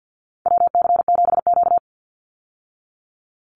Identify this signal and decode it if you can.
Morse code, heard as a keyed tone, about 50 wpm, 700 Hz. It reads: WC7Q
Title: Call Signs Repeated 2x - 50wpm